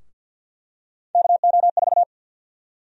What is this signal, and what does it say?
Morse code, heard as a keyed tone, about 50 wpm, 700 Hz. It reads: KO4
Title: USA Call Sign Prefixes - 50wpm